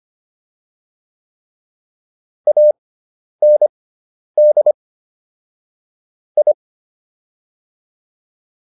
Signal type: Morse code

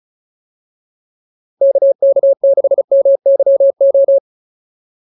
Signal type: Morse code

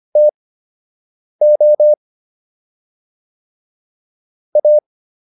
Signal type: Morse code

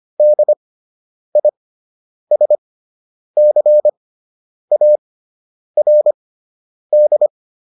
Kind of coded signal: Morse code

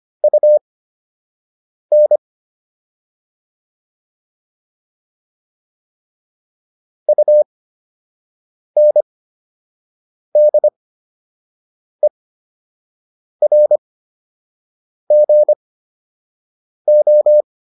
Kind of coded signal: Morse code

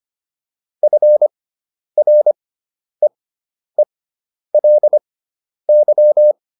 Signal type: Morse code

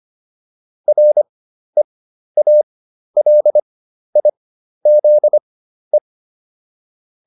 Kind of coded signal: Morse code

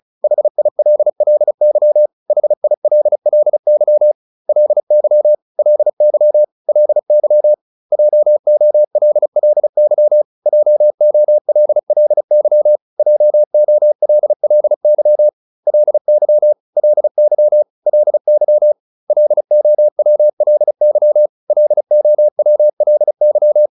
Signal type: Morse code

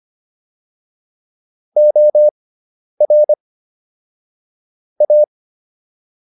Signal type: Morse code

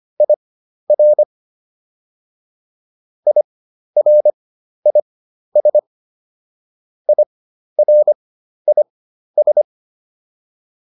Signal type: Morse code

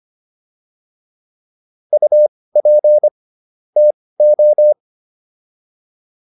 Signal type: Morse code

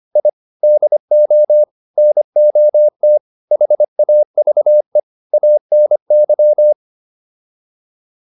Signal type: Morse code

